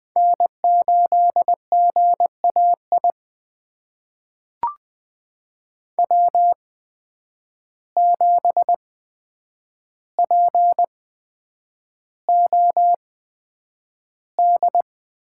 Morse code, heard as a keyed tone, about 20 wpm, spaced as for 9 wpm, 700 Hz. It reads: N8GAI W7POD
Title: USA Call Signs - 5 Characters - 8x Character Spacing - 20wpm